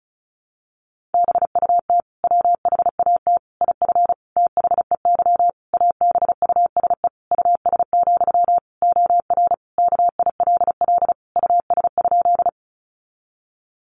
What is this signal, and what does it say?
Morse code, heard as a keyed tone, about 34 wpm, 700 Hz. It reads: BUT WHAT IF THEY ABUSE US, OR KILL US?
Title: Sentences from Top 1400 Words 35wpm